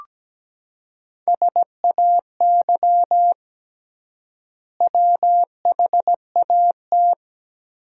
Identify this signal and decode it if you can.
Morse code, heard as a keyed tone, about 17 wpm, 700 Hz. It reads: SAY WHAT
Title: 3x Word-Spacing - Sentences from Top 200 Words - 17wpm